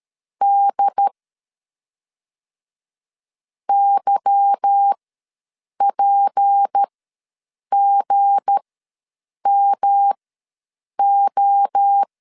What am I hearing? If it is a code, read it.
Morse code, heard as a keyed tone, about 13 wpm, spaced as for 7 wpm, 800 Hz. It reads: D YPGMO
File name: LEZIONI / 8 YJ45 / 8 Y ws 12wpm 3409